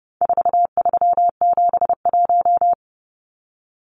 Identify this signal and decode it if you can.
Morse code, heard as a keyed tone, about 30 wpm, 700 Hz. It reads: 4371
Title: Numbers - 4-Digit 30wpm